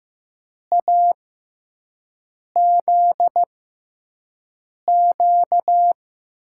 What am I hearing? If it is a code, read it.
Morse code, heard as a keyed tone, about 15 wpm, spaced as for 5 wpm, 700 Hz. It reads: AZQ
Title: USA Call Signs - 6 Characters - 6x Character Spacing - 15wpm